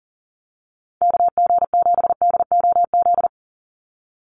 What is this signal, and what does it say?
Morse code, heard as a keyed tone, about 40 wpm, 700 Hz. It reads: KG7DOZ
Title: USA Call Signs 40wpm